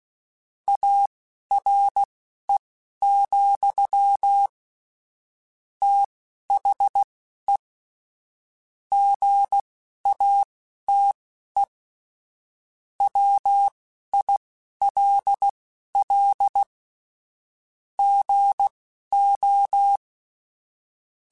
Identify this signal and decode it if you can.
Morse code, heard as a keyed tone, about 16 wpm, spaced as for 11 wpm, 800 Hz. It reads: ARE, THE GATE WILL GO
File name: Technical Subjects / Technical Subjects-mp3s / OR Gate @10